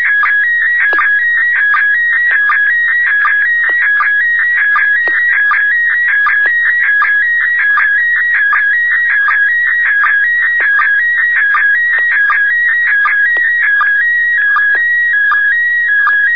SSTV (slow-scan television), PD 180.